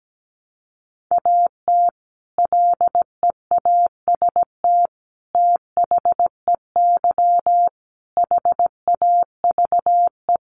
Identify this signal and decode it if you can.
Morse code, heard as a keyed tone, about 17 wpm, 700 Hz. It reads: AT LEAST THEY HAVE
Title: Sentences from Top 1500 Words 17wpm